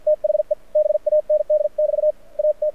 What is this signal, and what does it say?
Morse code, heard as a keyed tone, about 52 wpm, spaced as for 45 wpm, 600 Hz. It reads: THE BAND <BT> AT
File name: archived / CRSnet